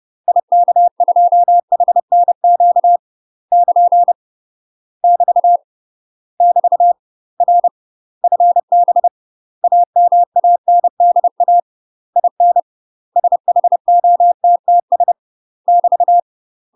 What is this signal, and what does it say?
Morse code, heard as a keyed tone, about 30 wpm, 700 Hz. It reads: IK2HNQ <KN> <BT> <BT> R FB AMANDA IN SHOTTS <BT>
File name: UK MASTER QSOs TRAINING / 30WPM / UK MASTER QSOs 0041